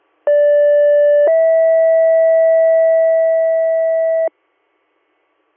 Two-tone page, heard as a one-second tone, then a three-second tone, about 580 then 650 Hz. It is heard from 0.3 to 4.3 s.